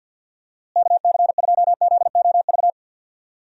Morse code, heard as a keyed tone, about 52 wpm, 700 Hz. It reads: KC2ZOV
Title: USA Call Signs - 6 Characters - Repeated 2x - 50wpm